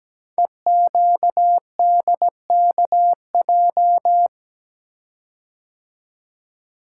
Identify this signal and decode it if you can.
Morse code, heard as a keyed tone, about 17 wpm, 700 Hz. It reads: EQDKJ
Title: Code Groups - 5 Letters - 17wpm